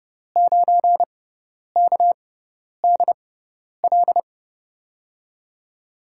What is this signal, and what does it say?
Morse code, heard as a keyed tone, about 30 wpm, spaced as for 10 wpm, 700 Hz. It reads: 9KDL
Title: USA Call Signs - 5 Characters - 6x Character Spacing - 30wpm